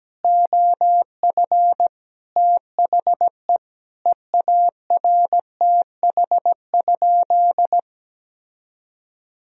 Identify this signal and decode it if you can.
Morse code, heard as a keyed tone, about 17 wpm, 700 Hz. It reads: OF THE EARTH?
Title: Sentences from Top 1000 Words - Review Part 3 - 17wpm